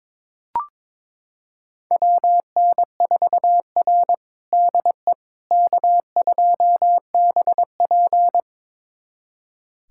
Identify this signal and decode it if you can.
Morse code, heard as a keyed tone, about 21 wpm, 700 Hz. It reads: WN4R DE K2BP
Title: USA Callsign DE Callsign - 4 Characters - 22wpm